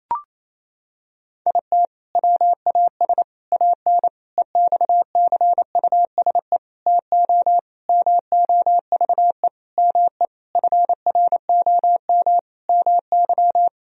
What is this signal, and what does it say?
Morse code, heard as a keyed tone, about 28 wpm, 700 Hz. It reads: IT WAS AN EXCUSE TO MOVE ME FROM MY
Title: Sentences from Top 1000 Words - Review Part 1 - 28wpm